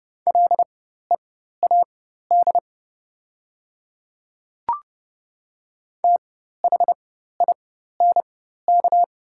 Morse code, heard as a keyed tone, about 30 wpm, spaced as for 14 wpm, 700 Hz. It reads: LEAD THINK
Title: 4x Character Spacing - Top 500 Words - 30wpm